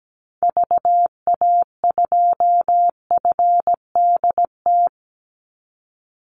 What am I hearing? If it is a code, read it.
Morse code, heard as a keyed tone, about 17 wpm, 700 Hz. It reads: VA2FDT
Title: Call Signs - 17wpm